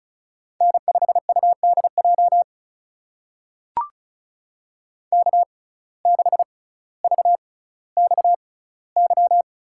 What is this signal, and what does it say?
Morse code, heard as a keyed tone, about 36 wpm, spaced as for 18 wpm, 700 Hz. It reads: N5UDJ K6VXY
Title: USA Call Signs - 5 Characters - 6x Character Spacing - 35wpm